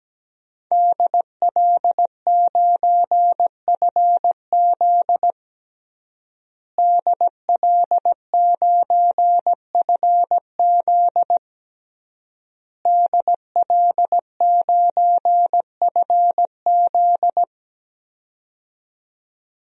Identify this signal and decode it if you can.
Morse code, heard as a keyed tone, about 17 wpm, 700 Hz. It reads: DL9FZ DL9FZ DL9FZ
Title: DXCC-100 Call Signs - 5 Characters - Repeated 3x - 17wpm